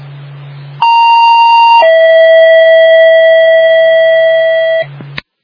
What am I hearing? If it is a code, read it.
Two-tone page, heard as a one-second tone, then a three-second tone, about 930 then 650 Hz. It is heard from 0.8 to 4.8 s.